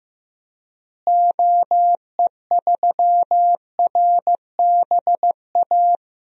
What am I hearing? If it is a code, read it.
Morse code, heard as a keyed tone, about 15 wpm, 700 Hz. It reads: OE3RBA
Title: DXCC-100 Call Signs - 6 Characters - Repeated 3x - 15wpm